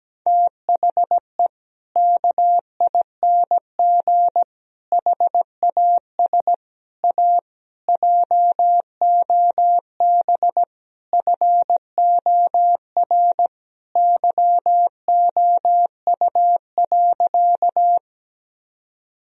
Morse code, heard as a keyed tone, about 17 wpm, 700 Hz. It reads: THE KING HAS A JOB FOR YOU.
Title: Sentences from Top 1000 Words - Review Part 3 - 17wpm